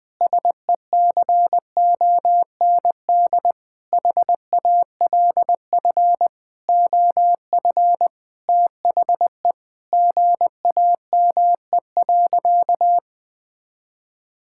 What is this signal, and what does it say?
Morse code, heard as a keyed tone, about 19 wpm, 700 Hz. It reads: SECOND HALF OF THE GAME.
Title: Sentences with 6-Word Phrases - 20wpm